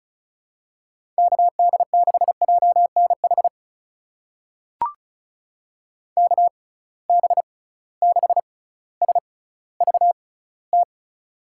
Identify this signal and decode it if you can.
Morse code, heard as a keyed tone, about 36 wpm, spaced as for 18 wpm, 700 Hz. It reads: KD6JNH KB6SVT
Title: USA Call Signs - 6 Characters - 6x Character Spacing - 35wpm